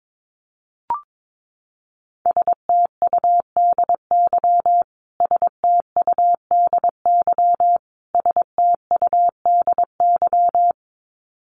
Morse code, heard as a keyed tone, about 22 wpm, 700 Hz. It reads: STUDY STUDY STUDY
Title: Top 500 Words Repeated 22wpm